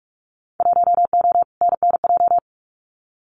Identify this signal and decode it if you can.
Morse code, heard as a keyed tone, about 44 wpm, 700 Hz. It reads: 1O NNJ
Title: Winter Field Day Exchanges 45wpm